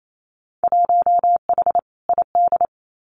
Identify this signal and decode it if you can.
Morse code, heard as a keyed tone, about 28 wpm, 700 Hz. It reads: 1H ID
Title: Winter Field Day Exchanges 28wpm